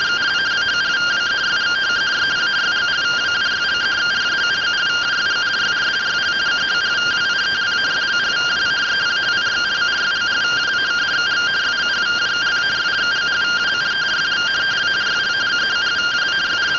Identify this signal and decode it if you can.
Two-tone teleprinter signal, about 170 Hz shift.